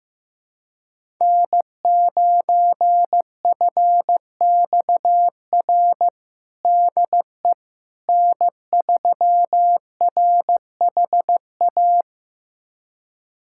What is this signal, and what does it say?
Morse code, heard as a keyed tone, about 16 wpm, 700 Hz. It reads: N9FXR DE N3RHA
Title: USA Callsign DE Callsign - 5 Characters - 15wpm